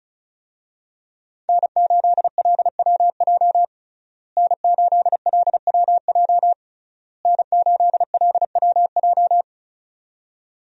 Morse code, heard as a keyed tone, about 36 wpm, 700 Hz. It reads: N8LWJ N8LWJ N8LWJ
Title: USA Call Signs - 5 Characters - Repeated 3x - 35wpm